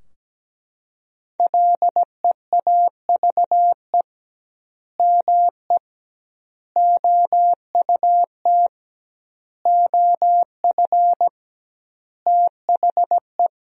Morse code, heard as a keyed tone, about 17 wpm, 700 Hz. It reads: LEAVE ME OUT OF THE
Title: 2x Word-Spacing - Sentences from Top 400 Words - 17wpm